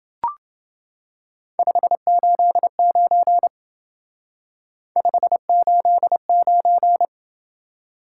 Morse code, heard as a keyed tone, about 28 wpm, 700 Hz. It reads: 589 589